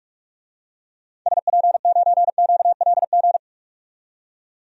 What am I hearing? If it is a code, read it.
Morse code, heard as a keyed tone, about 46 wpm, 700 Hz. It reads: SP9QLG
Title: DXCC-100 Call Signs - 6 Characters - 45wpm